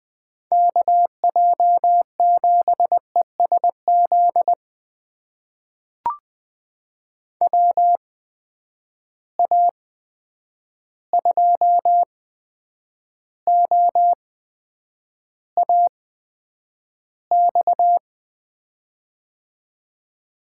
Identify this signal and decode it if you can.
Morse code, heard as a keyed tone, about 20 wpm, spaced as for 9 wpm, 700 Hz. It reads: KJ7ESZ WA2OAX
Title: USA Call Signs - 6 Characters - 8x Character Spacing - 20wpm